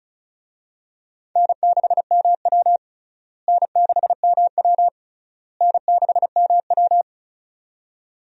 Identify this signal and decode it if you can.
Morse code, heard as a keyed tone, about 36 wpm, 700 Hz. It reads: N6MW N6MW N6MW
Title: USA Call Signs - 4 Characters - Repeated 3x - 35wpm